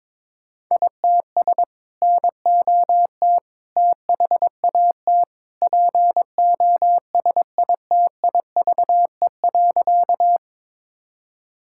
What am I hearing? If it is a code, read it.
Morse code, heard as a keyed tone, about 22 wpm, 700 Hz. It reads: ITS NOT THAT POSITIVE.
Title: Sentences from Top 1000 Words - Review Part 5 - 22wpm